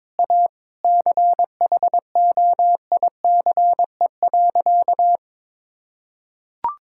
Morse code, heard as a keyed tone, about 22 wpm, 700 Hz. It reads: A CHOICE.
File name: practice / 22wpm / Sentences with 4-Word Phrases - 22wpm